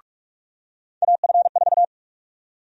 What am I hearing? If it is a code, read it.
Morse code, heard as a keyed tone, about 46 wpm, 700 Hz. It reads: AF4
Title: USA Call Sign Prefixes - 45wpm